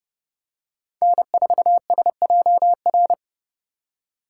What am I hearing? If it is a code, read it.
Morse code, heard as a keyed tone, about 30 wpm, 700 Hz. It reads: N4SJR